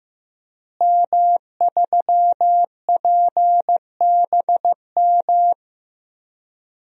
Morse code, heard as a keyed tone, about 15 wpm, 700 Hz. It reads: M3PBM
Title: DXCC-100 Call Signs - 5 Characters - Repeated 2x - 15wpm